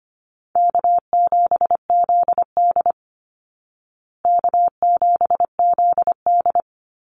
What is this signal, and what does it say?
Morse code, heard as a keyed tone, about 25 wpm, 700 Hz. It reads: K7ZD K7ZD
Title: Easy Call Signs Repeated 2x - 25wpm